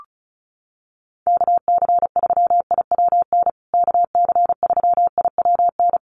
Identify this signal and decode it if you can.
Morse code, heard as a keyed tone, about 34 wpm, 700 Hz. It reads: KC3IWN KC3IWN
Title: USA Call Signs Repeated 2x 35wpm